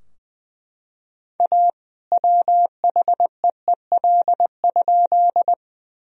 Morse code, heard as a keyed tone, about 20 wpm, 700 Hz. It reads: A WHEEL?
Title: Sentences from Top 1000 Words - Review Part 3 - 20wpm